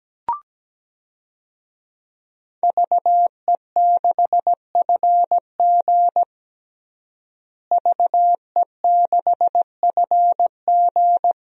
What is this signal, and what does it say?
Morse code, heard as a keyed tone, about 17 wpm, 700 Hz. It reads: VE6FG VE6FG
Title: DXCC-100 Call Signs - 5 Characters - Repeated 3x - 17wpm